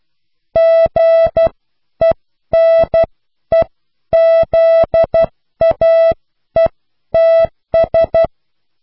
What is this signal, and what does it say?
Morse code, heard as a keyed tone, about 12 wpm, spaced as for 10 wpm, 660 Hz. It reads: GENEZAETS
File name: cw / pseudo worden(10)